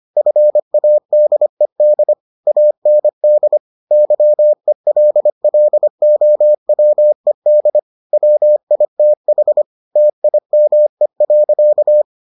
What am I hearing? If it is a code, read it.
Morse code, heard as a keyed tone, about 25 wpm, 600 Hz. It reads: FADED AND YELLOWED WITH TIME.